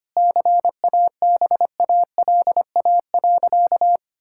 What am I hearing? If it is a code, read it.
Morse code, heard as a keyed tone, about 25 wpm, 700 Hz. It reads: CABALA.
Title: Pangram Challenge v2 - 25wpm